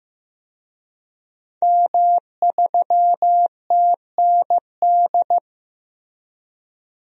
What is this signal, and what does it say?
Morse code, heard as a keyed tone, about 15 wpm, 700 Hz. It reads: M3TND
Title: DXCC-100 Call Signs - 5 Characters - Repeated 2x - 15wpm